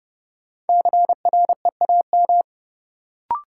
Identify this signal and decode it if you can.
Morse code, heard as a keyed tone, about 31 wpm, 700 Hz. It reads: CREAM
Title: Sentences from Top 1500 Words - New Word Drill - 30wpm